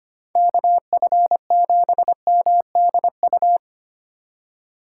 Morse code, heard as a keyed tone, about 25 wpm, 700 Hz. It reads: KF7MDU
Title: USA Call Signs - 6 Characters - 3x Character Spacing - 25wpm